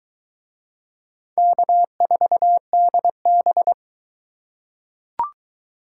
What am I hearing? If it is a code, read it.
Morse code, heard as a keyed tone, about 21 wpm, 700 Hz. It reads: K4DB